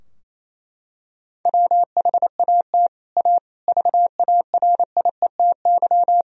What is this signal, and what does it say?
Morse code, heard as a keyed tone, about 28 wpm, 700 Hz. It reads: WHAT A VARIETY
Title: Sentences from Top 1000 Words - Review Part 4 - 28wpm